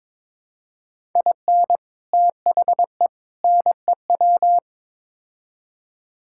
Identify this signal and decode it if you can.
Morse code, heard as a keyed tone, about 22 wpm, 700 Hz. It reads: IN THE NEW
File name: practice / 22wpm / Build A Phrase - 5-Word Phrases - 22wpm